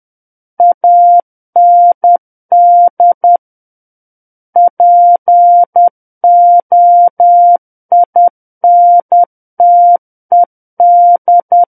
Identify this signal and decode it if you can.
Morse code, heard as a keyed tone, about 10 wpm, 700 Hz. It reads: AND POINTED
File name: Book One / 01-Chapter 1 - 10 WPM